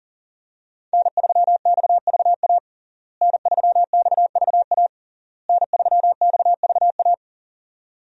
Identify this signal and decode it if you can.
Morse code, heard as a keyed tone, about 40 wpm, 700 Hz. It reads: N3XVA N3XVA N3XVA
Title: USA Call Signs - 5 Characters - Repeated 3x - 40wpm